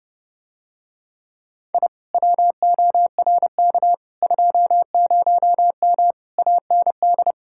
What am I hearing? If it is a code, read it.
Morse code, heard as a keyed tone, about 29 wpm, 700 Hz. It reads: I WORK 20M AND